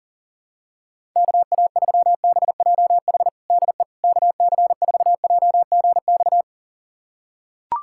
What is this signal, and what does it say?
Morse code, heard as a keyed tone, about 40 wpm, 700 Hz. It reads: KA3BJH DE KC4JGX
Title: USA Callsign DE Callsign - 6 Characters - 40wpm